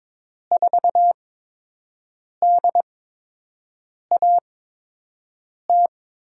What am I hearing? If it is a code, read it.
Morse code, heard as a keyed tone, about 22 wpm, 700 Hz. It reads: 4 D A T
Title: USA Call Signs - 5 Characters - 8x Character Spacing - 22wpm